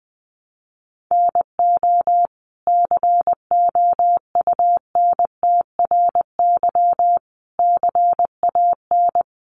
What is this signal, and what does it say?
Morse code, heard as a keyed tone, about 20 wpm, 700 Hz. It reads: NO COUNTRY CAN
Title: Sentences from Top 800 Words 20wpm